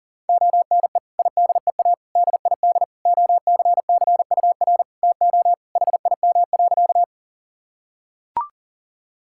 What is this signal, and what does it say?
Morse code, heard as a keyed tone, about 40 wpm, 700 Hz. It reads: ONE IDEA DID OCCUR TO HIM.
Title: Sentences from Top 1000 Words - Review Part 6 - 40wpm